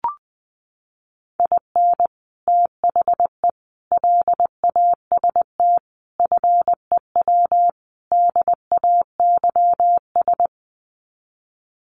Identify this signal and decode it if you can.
Morse code, heard as a keyed tone, about 20 wpm, 700 Hz. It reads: IN THE LAST FEW DAYS
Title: Sets of 5 Words Encore2 20wpm